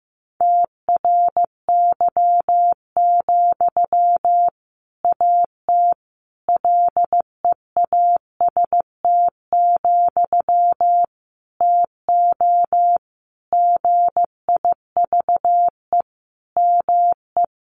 Morse code, heard as a keyed tone, about 15 wpm, 700 Hz. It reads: TRY, AT LEAST, TO GIVE ME
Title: Sentences from Top 700 Words 15wpm